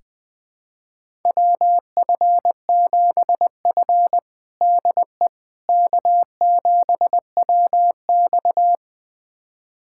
Morse code, heard as a keyed tone, about 19 wpm, 700 Hz. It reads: WF7F DE K7WX
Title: USA Callsign DE Callsign - 4 Characters - 20wpm